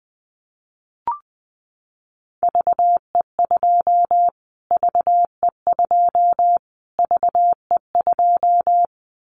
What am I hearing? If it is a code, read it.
Morse code, heard as a keyed tone, about 20 wpm, 700 Hz. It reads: VE2 VE2 VE2
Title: Callsign Prefixes for Top 25 DXCC - Repeated 3x - 20wpm